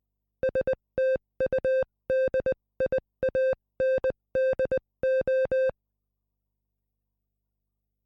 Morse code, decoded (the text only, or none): STUDIANDO